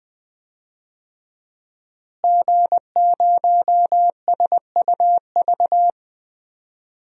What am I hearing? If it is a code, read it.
Morse code, heard as a keyed tone, about 20 wpm, 700 Hz. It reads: G0SUV
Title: DXCC-100 Call Signs - 5 Characters - Repeated 2x - 20wpm